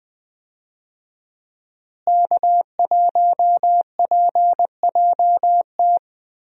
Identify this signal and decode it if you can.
Morse code, heard as a keyed tone, about 20 wpm, 700 Hz. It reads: K1PJT
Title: USA Call Signs - 5 Characters - Repeated 3x - 20wpm